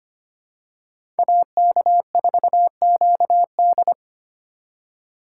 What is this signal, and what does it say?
Morse code, heard as a keyed tone, about 25 wpm, 700 Hz. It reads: AK4QD
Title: USA Call Signs - 5 Characters - 4x Character Spacing - 25wpm